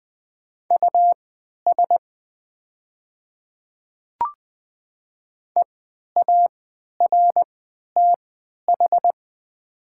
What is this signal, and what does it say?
Morse code, heard as a keyed tone, about 20 wpm, spaced as for 11 wpm, 700 Hz. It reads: US EARTH
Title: 3x Character Spacing - Top 200 Words - 20wpm